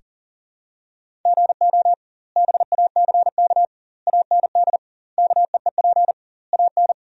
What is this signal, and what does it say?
Morse code, heard as a keyed tone, about 40 wpm, 700 Hz. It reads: GO BACK AND KEEP AN
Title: 2x Word-Spacing - Sentences from Top 300 Words - 40wpm